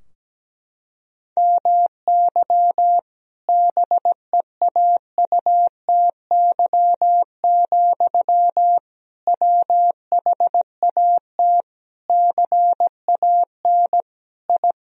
Morse code, heard as a keyed tone, about 17 wpm, 700 Hz. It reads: MY BEAUTY, WHAT CAN I
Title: Sentences from Top 1000 Words - Review Part 2 - 17wpm